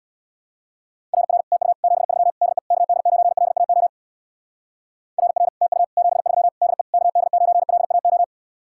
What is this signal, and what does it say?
Morse code, heard as a keyed tone, about 64 wpm, 700 Hz. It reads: RR TU 73 DE KD9LAO RR TU 73 DE KD9LAO